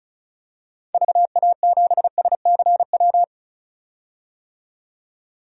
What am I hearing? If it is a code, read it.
Morse code, heard as a keyed tone, about 36 wpm, 700 Hz. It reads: VA7SCW
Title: DXCC-100 Call Signs - 6 Characters - 35wpm